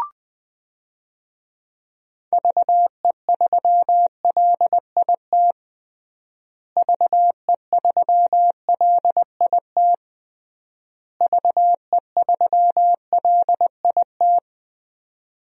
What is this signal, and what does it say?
Morse code, heard as a keyed tone, about 20 wpm, 700 Hz. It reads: VE3LIT VE3LIT VE3LIT
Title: DXCC-100 Call Signs - 6 Characters - Repeated 3x - 20wpm